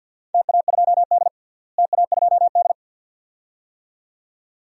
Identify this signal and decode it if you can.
Morse code, heard as a keyed tone, about 47 wpm, 700 Hz. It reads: TA2D TA2D
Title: DXCC-100 Call Signs - 4 Characters - Repeated 2x - 50wpm